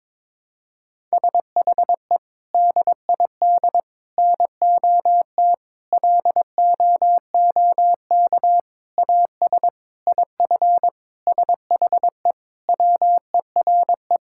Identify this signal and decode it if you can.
Morse code, heard as a keyed tone, about 22 wpm, 700 Hz. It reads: SHE DID NOT LOOK AS IF SHE WERE
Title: Sentences from Top 1000 Words - Review Part 6 - 22wpm